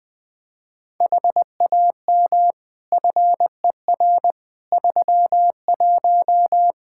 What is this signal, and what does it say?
Morse code, heard as a keyed tone, about 21 wpm, 700 Hz. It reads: HAM FER 31
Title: Practice QSO - Exchange 3 - More Info - 20wpm